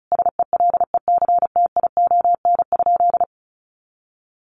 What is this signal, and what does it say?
Morse code, heard as a keyed tone, about 34 wpm, 700 Hz. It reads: SELECTION?
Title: Sentences from Top 1400 Words 35wpm